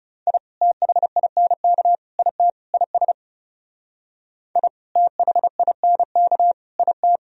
Morse code, heard as a keyed tone, about 38 wpm, 700 Hz. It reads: I THINK IT IS I THINK IT